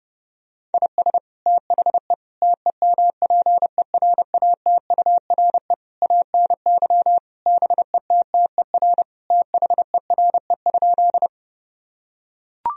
Morse code, heard as a keyed tone, about 32 wpm, 700 Hz. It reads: IS THE TEMPERATURE ANY BETTER THERE?
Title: Sentences from Top 1000 Words - Review Part 4 - 30wpm